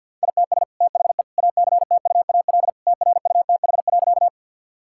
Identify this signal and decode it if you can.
Morse code, heard as a keyed tone, about 52 wpm, 700 Hz. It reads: ITS THE ACTUAL TRUTH.
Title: Sentences from Top 1000 Words - Review Part 5 - 50wpm